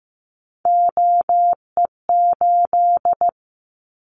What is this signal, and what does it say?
Morse code, heard as a keyed tone, about 15 wpm, 700 Hz. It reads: OE8
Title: Callsign Prefixes for Top 100 DXCC - 15wpm